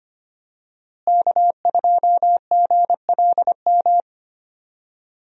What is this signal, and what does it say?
Morse code, heard as a keyed tone, about 25 wpm, 700 Hz. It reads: K2GLM
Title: USA Call Signs - 5 Characters - Repeated 3x - 25wpm